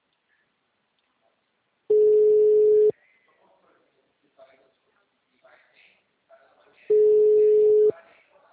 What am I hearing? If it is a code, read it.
Ringback tone (Europe, 425 Hz), with speech.
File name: out- -2030- - - .32081